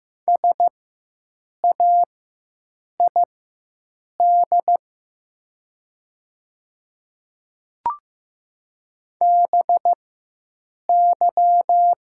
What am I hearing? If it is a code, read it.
Morse code, heard as a keyed tone, about 15 wpm, spaced as for 7 wpm, 700 Hz. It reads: SAID BY
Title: 4x Character Spacing - Top 100 Words - 15wpm